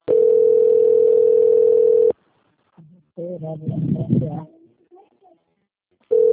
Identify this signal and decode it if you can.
Ringback tone (North America), 440 + 480 Hz, with speech.
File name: rg-1600-unknown- - - .51689